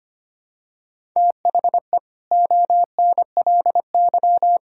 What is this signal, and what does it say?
Morse code, heard as a keyed tone, about 25 wpm, 700 Hz. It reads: THE ONLY
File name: practice / 25wpm / Sentences with 9-Word Phrases - 25wpm